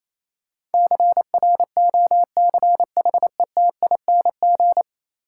Morse code, heard as a keyed tone, about 27 wpm, 700 Hz. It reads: CROCHETING
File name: Easy Words - Speed-Racing with Hints - 22wpm